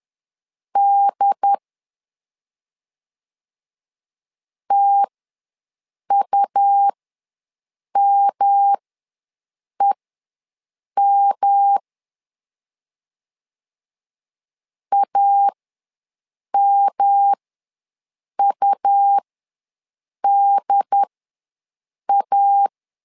Morse code, heard as a keyed tone, about 10 wpm, spaced as for 6 wpm, 800 Hz. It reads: D TUMEM AMUDA
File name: LEZIONI / 3 MWRO / 3 M ws 10wpm 2121